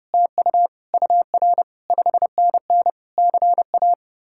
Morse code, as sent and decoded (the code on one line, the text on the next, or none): - ..- / ..- .-. / ..... -. -. / -.-. .-
TU UR 5NN CA